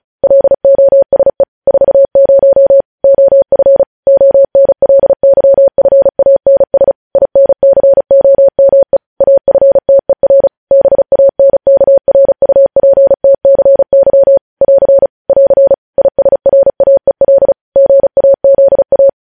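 Morse code, heard as a keyed tone, about 36 wpm, 550 Hz. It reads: LOSE 40 OF ONLYFANS INCOME AFTER BANKRUPTCY <AR> <AR> ISRAEL GAZA
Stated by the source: Morse code practice sending news headlines at 35 wpm.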